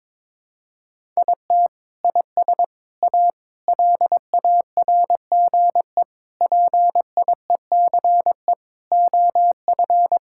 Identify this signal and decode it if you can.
Morse code, heard as a keyed tone, about 21 wpm, 700 Hz. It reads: IT IS A LARGE PIECE OF
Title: Sentences from Top 1000 Words - Review Part 5 - 22wpm